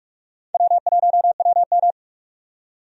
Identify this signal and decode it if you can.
Morse code, heard as a keyed tone, about 45 wpm, 700 Hz. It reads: W1WM
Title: USA Call Signs - 4 Characters - 45wpm